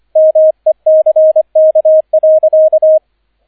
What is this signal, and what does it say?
Morse code, heard as a keyed tone, about 25 wpm, 630 Hz, with radio noise. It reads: MECK.
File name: 003 Dritter Streich 25wpm